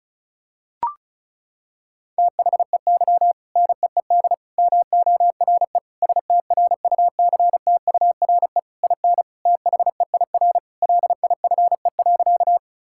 Morse code, heard as a keyed tone, about 36 wpm, 700 Hz. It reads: THEY NEED MORE STRUCTURE IN THEIR LIFE.
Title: Sentences from Top 1000 Words - Review Part 6 - 35wpm